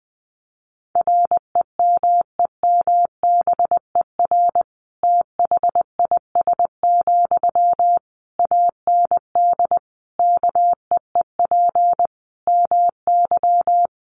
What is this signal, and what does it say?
Morse code, heard as a keyed tone, about 20 wpm, 700 Hz. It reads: REMEMBER THIS, AND KEEP MY
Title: Sentences from Top 1500 Words 20wpm